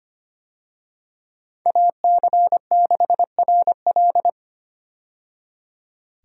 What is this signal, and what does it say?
Morse code, heard as a keyed tone, about 25 wpm, 700 Hz. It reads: AC6RL
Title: USA Call Signs - 5 Characters - 25wpm